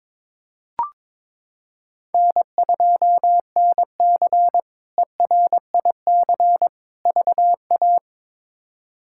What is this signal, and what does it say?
Morse code, heard as a keyed tone, about 21 wpm, 700 Hz. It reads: N2NC ERIC VA
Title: K1USN SST QSO Practice - 22wpm